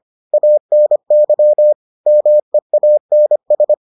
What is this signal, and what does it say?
Morse code, heard as a keyed tone, about 25 wpm, 600 Hz. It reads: ANY MEANS